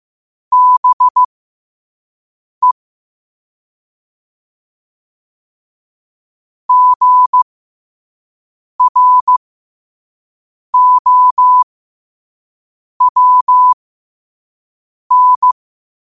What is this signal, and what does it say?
Morse code, heard as a keyed tone, about 15 wpm, spaced as for 5 wpm, 1000 Hz. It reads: BE GROWN
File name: qotdfiles / 5wpm